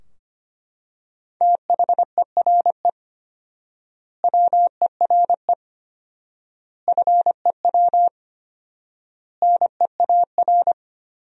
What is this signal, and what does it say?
Morse code, heard as a keyed tone, about 25 wpm, 700 Hz. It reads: THERE WERE FEW NEAR
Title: 4x Word-Spacing - Sentences from Top 300 Words - 25wpm